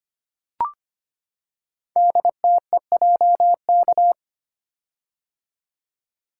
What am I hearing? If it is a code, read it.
Morse code, heard as a keyed tone, about 25 wpm, 700 Hz. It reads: DTEJK
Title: Code Groups - 5 Letters - 25wpm